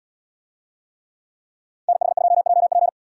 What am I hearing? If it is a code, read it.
Morse code, heard as a keyed tone, about 74 wpm, 700 Hz. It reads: NH2JP